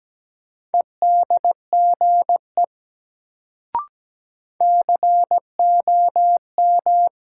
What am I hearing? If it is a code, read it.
Morse code, heard as a keyed tone, about 17 wpm, 700 Hz. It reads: EDGE COM
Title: Sentences from Top 1300 Words - New Word Drill Alt - 17wpm